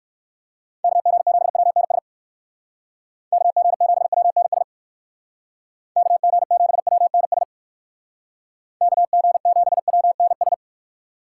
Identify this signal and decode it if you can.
Morse code, heard as a keyed tone, about 59 wpm, 700 Hz. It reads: KG7WNS KG7WNS KG7WNS KG7WNS